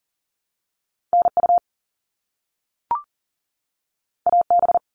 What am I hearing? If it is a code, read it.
Morse code, heard as a keyed tone, about 40 wpm, 700 Hz. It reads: NU AB
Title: Top Two Letter Combinations in Words 40wpm